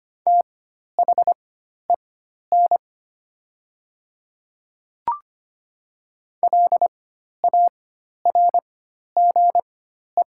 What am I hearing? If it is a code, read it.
Morse code, heard as a keyed tone, about 25 wpm, spaced as for 12 wpm, 700 Hz. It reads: THEN LARGE
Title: 4x Character Spacing - Top 200 Words - 25wpm